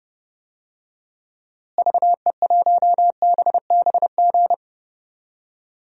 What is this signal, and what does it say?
Morse code, heard as a keyed tone, about 30 wpm, 700 Hz. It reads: VE1BBG
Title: DXCC-100 Call Signs - 6 Characters - Repeated 3x - 30wpm